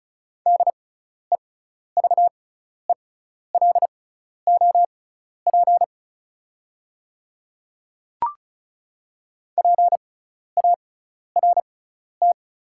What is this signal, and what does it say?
Morse code, heard as a keyed tone, about 36 wpm, spaced as for 12 wpm, 690 Hz. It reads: DEVELOP PART
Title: 6x Character Spacing - Top 500 Words - 35wpm